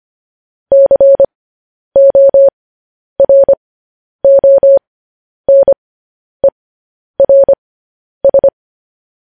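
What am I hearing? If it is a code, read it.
Morse code, heard as a keyed tone, about 25 wpm, spaced as for 10 wpm, 550 Hz. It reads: CORONERS
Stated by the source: Morse code news headlines practice at 10 wpm.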